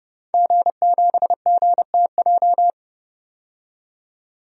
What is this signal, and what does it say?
Morse code, heard as a keyed tone, about 30 wpm, 700 Hz. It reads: G7GTJ
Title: DXCC-100 Call Signs - 5 Characters - 30wpm